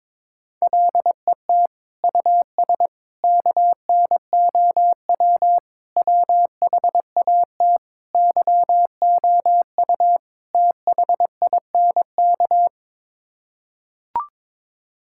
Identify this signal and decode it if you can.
Morse code, heard as a keyed tone, about 22 wpm, 700 Hz. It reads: LET US KNOW WHAT YOU THINK
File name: practice / 22wpm / Sets of 6 Words - 22wpm